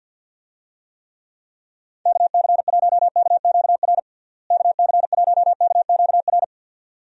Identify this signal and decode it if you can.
Morse code, heard as a keyed tone, about 51 wpm, 700 Hz. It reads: KC1KQR KC1KQR
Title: USA Call Signs - 6 Characters - Repeated 2x - 50wpm